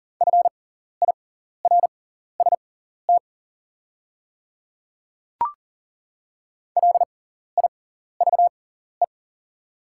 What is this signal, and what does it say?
Morse code, heard as a keyed tone, about 40 wpm, spaced as for 14 wpm, 700 Hz. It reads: FIRST LIVE
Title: 6x Character Spacing - Top 200 Words - 40wpm